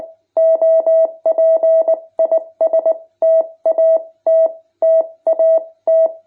Morse code, heard as a keyed tone, about 20 wpm, spaced as for 17 wpm, 650 Hz. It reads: OP IS TAT TAT
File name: jh8cbh / 2021cw